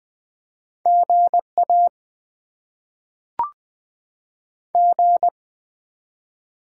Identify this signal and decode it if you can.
Morse code, heard as a keyed tone, about 20 wpm, 700 Hz. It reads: GA G
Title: Top 20 QSO Elements - 8x Character Spacing - 20wpm